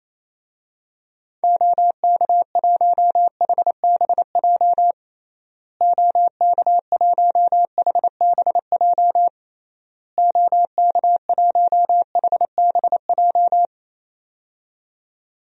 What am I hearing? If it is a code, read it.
Morse code, heard as a keyed tone, about 28 wpm, 700 Hz. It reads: OK1HBJ OK1HBJ OK1HBJ
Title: DXCC-100 Call Signs - 6 Characters - Repeated 3x - 28wpm